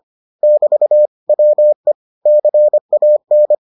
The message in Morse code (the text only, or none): <BT> WE CAN